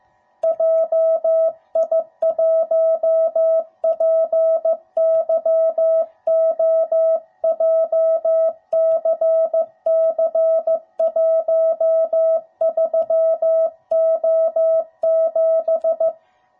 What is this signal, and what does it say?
Morse code, heard as a keyed tone, about 14 wpm, 650 Hz. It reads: JI1PYO JCC13O7
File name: jh8cbh / 2021cw / cw05b